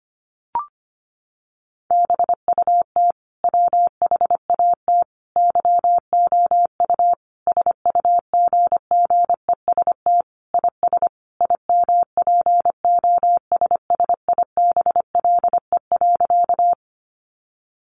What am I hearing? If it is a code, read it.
Morse code, heard as a keyed tone, about 25 wpm, 700 Hz. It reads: BUT WHAT YOU SUGGEST IS IMPOSSIBLE.
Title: Sentences from Top 1300 Words 25wpm